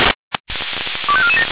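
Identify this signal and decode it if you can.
Five-tone selective call (ZVEI), heard as quick runs of five tones, tones of about 70 ms. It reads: 25447
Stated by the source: ZVEI five-tone selective call received on SDR.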